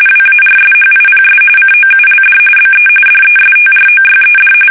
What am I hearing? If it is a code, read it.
RTTY, about 45.45 baud. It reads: WARSHIPS COULD BE DESTROYED
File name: fsk850